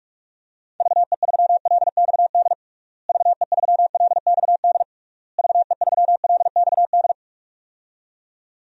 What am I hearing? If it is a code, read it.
Morse code, heard as a keyed tone, about 41 wpm, 700 Hz. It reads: VE3LXD VE3LXD VE3LXD
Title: DXCC-100 Call Signs - 6 Characters - Repeated 3x - 45wpm